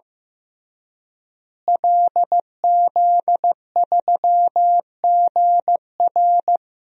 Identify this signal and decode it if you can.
Morse code, heard as a keyed tone, about 15 wpm, 700 Hz. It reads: LZ3GR
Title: DXCC-100 Call Signs - 5 Characters - Repeated 3x - 15wpm